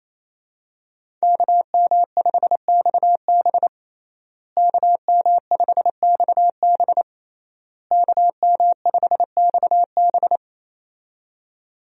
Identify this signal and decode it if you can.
Morse code, heard as a keyed tone, about 28 wpm, 700 Hz. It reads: KM5XB KM5XB KM5XB
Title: USA Call Signs - 5 Characters - Repeated 3x - 28wpm